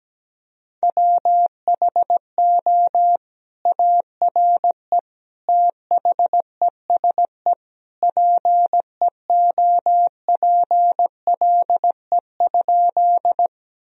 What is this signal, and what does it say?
Morse code, heard as a keyed tone, about 17 wpm, 700 Hz. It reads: WHO ARE THESE PEOPLE?
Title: Sentences from Top 1000 Words - Review Part 1 - 17wpm